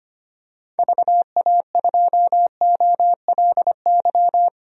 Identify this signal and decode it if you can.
Morse code, heard as a keyed tone, about 25 wpm, 700 Hz. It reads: VA2OLY
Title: DXCC-100 Call Signs - 6 Characters - Repeated 2x - 25wpm